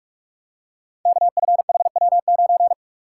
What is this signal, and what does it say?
Morse code, heard as a keyed tone, about 44 wpm, 690 Hz. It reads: KFHW9
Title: Code Groups - 5 Letter:Numbers - 45wpm